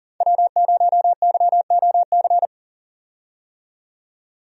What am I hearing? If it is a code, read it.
Morse code, heard as a keyed tone, about 40 wpm, 700 Hz. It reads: W0YOC
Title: USA Call Signs - 5 Characters - 40wpm